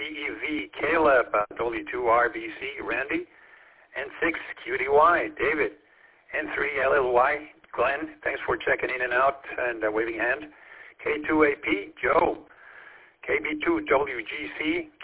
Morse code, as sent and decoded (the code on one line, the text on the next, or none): none